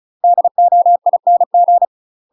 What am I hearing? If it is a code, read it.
Morse code, heard as a keyed tone, about 34 wpm, 700 Hz. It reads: DOING
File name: UK MASTER QSOs TRAINING / 35WPM / UK MASTER QSOs 0005